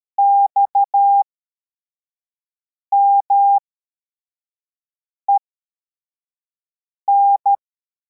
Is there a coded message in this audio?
Yes: Morse code from 0.2 to 7.5 s, heard as a keyed tone, about 13 wpm, spaced as for 4 wpm, 800 Hz. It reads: XMEN